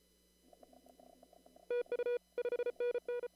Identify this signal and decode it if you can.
Morse code, heard as a keyed tone, about 31 wpm, 500 Hz. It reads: TU 5NN
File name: CW / kt5j wpx cw-op-k5pi